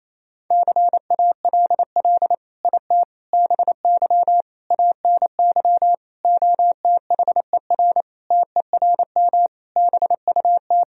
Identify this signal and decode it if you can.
Morse code, heard as a keyed tone, about 28 wpm, 700 Hz. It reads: CALL IT BY ANY OTHER TERM BUT
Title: Sentences from Top 1000 Words - Review Part 4 - 28wpm